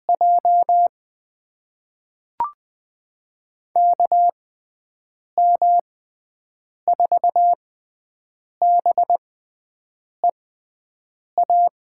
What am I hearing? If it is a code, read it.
Morse code, heard as a keyed tone, about 20 wpm, spaced as for 7 wpm, 700 Hz. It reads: J KM4BEA
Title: USA Call Signs - 6 Characters - 6x Character Spacing - 20wpm